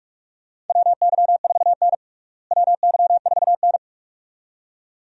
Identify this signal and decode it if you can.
Morse code, heard as a keyed tone, about 46 wpm, 700 Hz. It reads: WY4N WY4N
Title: USA Call Signs - 4 Characters - Repeated 2x - 45wpm